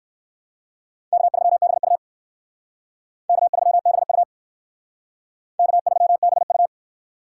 Morse code, heard as a keyed tone, about 59 wpm, 700 Hz. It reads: K3BU K3BU K3BU